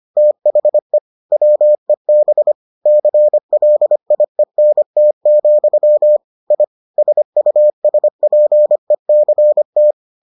Morse code, heard as a keyed tone, about 25 wpm, 600 Hz. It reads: THE WEB CLIENT, I SUSPECT